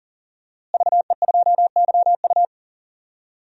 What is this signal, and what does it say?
Morse code, heard as a keyed tone, about 38 wpm, 700 Hz. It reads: VE2YU
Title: DXCC-100 Call Signs - 5 Characters - Repeated 2x - 40wpm